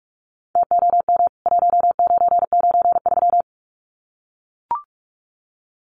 Morse code, heard as a keyed tone, about 44 wpm, 700 Hz. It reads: TOM 1993
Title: CWT Exchange 45wpm